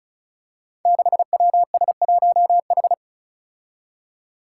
Morse code, heard as a keyed tone, about 36 wpm, 700 Hz. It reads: 6WS1H
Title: Five Challenging Letter-Number Combinations 35wpm